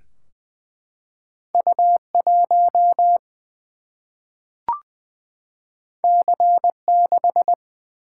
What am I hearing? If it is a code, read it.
Morse code, heard as a keyed tone, about 20 wpm, 700 Hz. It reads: U1 C6
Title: Single-Letter Single-Number 20wpm